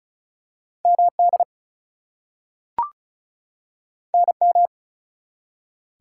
Morse code, heard as a keyed tone, about 36 wpm, 700 Hz. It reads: MD NM
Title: US States and Canadian Provinces 35wpm